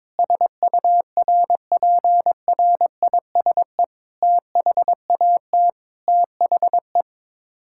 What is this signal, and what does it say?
Morse code, heard as a keyed tone, about 22 wpm, 700 Hz. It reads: SURPRISE THAT THE
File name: Sets of 9 Words - 22wpm